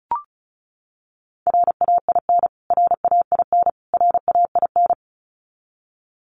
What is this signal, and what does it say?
Morse code, heard as a keyed tone, about 34 wpm, 700 Hz. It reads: RAIN RAIN RAIN
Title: Top 500 Words Repeated 35wpm